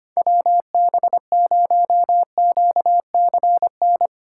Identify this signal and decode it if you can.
Morse code, heard as a keyed tone, about 25 wpm, 700 Hz. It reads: WB0QCN
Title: USA Call Signs - 6 Characters - 8x Character Spacing - 25wpm